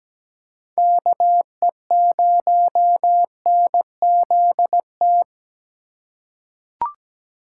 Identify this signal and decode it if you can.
Morse code, heard as a keyed tone, about 17 wpm, 700 Hz. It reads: KE0NZT
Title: USA Call Signs - 6 Characters - 2x Character Spacing - 17wpm